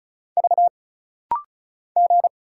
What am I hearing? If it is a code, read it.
Morse code, heard as a keyed tone, about 35 wpm, 700 Hz. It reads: V G
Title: Single Letter-Number Alt - Rapid-Fire 35wpm